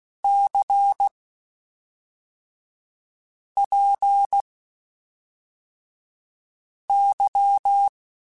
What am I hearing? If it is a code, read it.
Morse code, heard as a keyed tone, about 16 wpm, spaced as for 3 wpm, 800 Hz. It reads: CPY
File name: (spaced) / QSOs Spaced-mp3s / WD0BGP 1 @5 sp